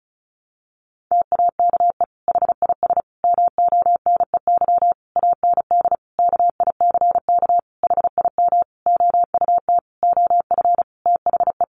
Morse code, heard as a keyed tone, about 34 wpm, 700 Hz. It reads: TAKE HIS MONEY AND KICK HIM OUT OF THE
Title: Sentences from Top 1300 Words 35wpm